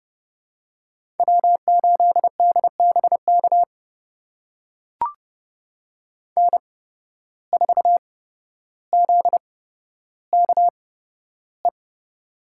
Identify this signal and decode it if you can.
Morse code, heard as a keyed tone, about 30 wpm, spaced as for 13 wpm, 700 Hz. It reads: W8DBK N4ZKE
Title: USA Call Signs - 5 Characters - 8x Character Spacing - 30wpm